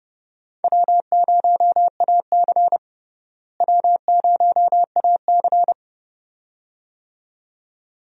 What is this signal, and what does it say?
Morse code, heard as a keyed tone, about 30 wpm, 700 Hz. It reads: W0AC W0AC
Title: USA Call Signs - 4 Characters - Repeated 2x - 30wpm